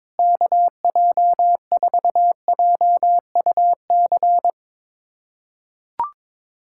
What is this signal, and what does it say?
Morse code, heard as a keyed tone, about 22 wpm, 700 Hz. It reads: KJ4JUC
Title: USA Callsign DE Callsign 22wpm